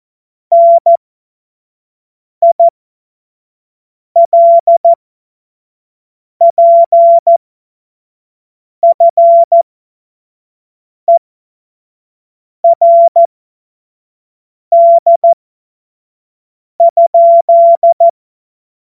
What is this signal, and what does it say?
Morse code, heard as a keyed tone, about 14 wpm, spaced as for 5 wpm, 680 Hz. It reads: NILPFERD?
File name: Morsekurs / Witze / Witz 16. Februar 2021, 25 F70